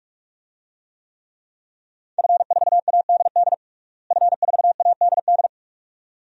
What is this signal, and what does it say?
Morse code, heard as a keyed tone, about 41 wpm, 700 Hz. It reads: F4ADD F4ADD
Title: DXCC-100 Call Signs - 5 Characters - Repeated 2x - 45wpm